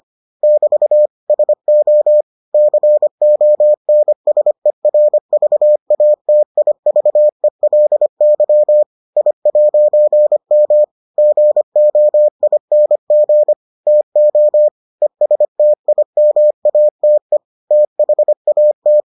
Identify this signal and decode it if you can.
Morse code, heard as a keyed tone, about 25 wpm, 600 Hz. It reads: <BT> SO CONSERVATIVELY I'M GOING TO ESTIMATE THAT